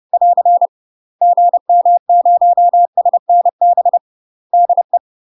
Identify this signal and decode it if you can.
Morse code, heard as a keyed tone, about 30 wpm, 700 Hz. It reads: <AR> GM0SNB DE
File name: UK MASTER QSOs TRAINING / 30WPM / UK MASTER QSOs 0016